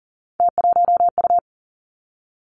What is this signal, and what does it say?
Morse code, heard as a keyed tone, about 41 wpm, 700 Hz. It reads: T1U
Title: Three Challenging Letter-Number Combinations 40wpm